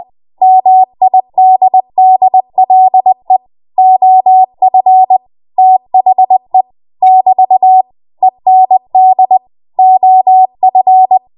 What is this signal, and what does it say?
Morse code, heard as a keyed tone, about 20 wpm, 750 Hz. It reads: MIDDLE OF THE <BT> END OF
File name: Morse / 20WPM